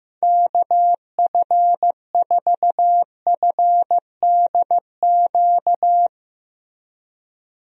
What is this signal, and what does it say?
Morse code, heard as a keyed tone, about 15 wpm, 700 Hz. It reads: KF4FDQ
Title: USA Call Signs - 6 Characters - Repeated 3x - 15wpm